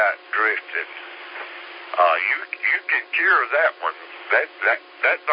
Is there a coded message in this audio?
Yes: Emergency Alert System (SAME) tones (header bursts) from 0.4 to 4.8 s, heard as harsh buzzing bursts, with speech.